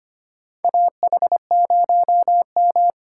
Morse code, heard as a keyed tone, about 25 wpm, 690 Hz. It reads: AH0M
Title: USA Call Signs - 4 Characters - 4x Character Spacing - 25wpm